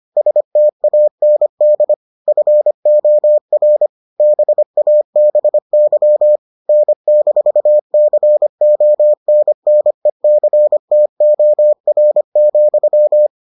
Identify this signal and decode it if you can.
Morse code, heard as a keyed tone, about 25 wpm, 600 Hz. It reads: STAND FOR BABY N-CONNECTOR,